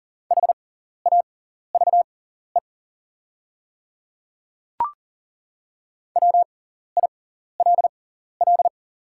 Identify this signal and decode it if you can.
Morse code, heard as a keyed tone, about 40 wpm, spaced as for 14 wpm, 700 Hz. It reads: HAVE WILL
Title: 6x Character Spacing - Top 200 Words - 40wpm